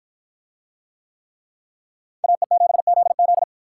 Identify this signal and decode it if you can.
Morse code, heard as a keyed tone, about 48 wpm, 700 Hz. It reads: AE7ZZ